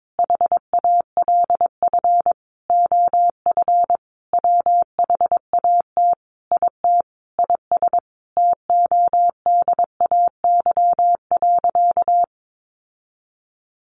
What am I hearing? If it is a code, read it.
Morse code, heard as a keyed tone, about 22 wpm, 700 Hz. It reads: HALF OF WHAT IT IS TODAY.
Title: Sentences from Top 1500 Words 22wpm